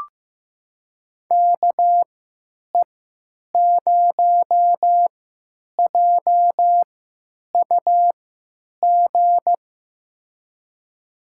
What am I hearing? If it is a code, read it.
Morse code, heard as a keyed tone, about 15 wpm, spaced as for 9 wpm, 700 Hz. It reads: KE0JUG
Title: USA Call Signs - 6 Characters - 3x Character Spacing - 15wpm